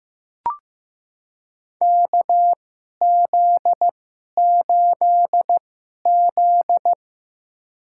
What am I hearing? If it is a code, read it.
Morse code, heard as a keyed tone, about 15 wpm, spaced as for 11 wpm, 700 Hz. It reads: KZ8Z
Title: USA Call Signs - 4 Characters - 2x Character Spacing - 15wpm